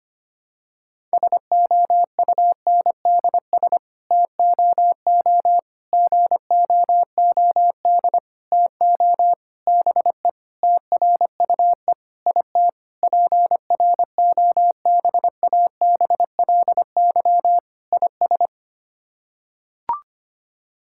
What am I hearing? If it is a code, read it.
Morse code, heard as a keyed tone, about 25 wpm, 700 Hz. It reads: SOUNDS TOO GOOD TO BE TRUE IT PROBABLY IS
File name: Sets of 9 Words - 25wpm